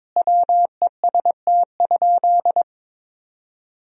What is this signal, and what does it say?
Morse code, heard as a keyed tone, about 22 wpm, 700 Hz. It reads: WEST?
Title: Sentences from Top 1000 Words - Review Part 2 - 22wpm